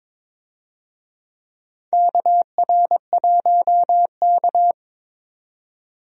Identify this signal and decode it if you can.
Morse code, heard as a keyed tone, about 22 wpm, 700 Hz. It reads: KR1K
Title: USA Call Signs - 4 Characters - 22wpm